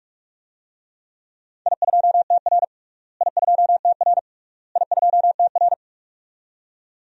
Morse code, heard as a keyed tone, about 41 wpm, 700 Hz. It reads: I2TR I2TR I2TR
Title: DXCC-100 Call Signs - 4 Characters - Repeated 3x - 45wpm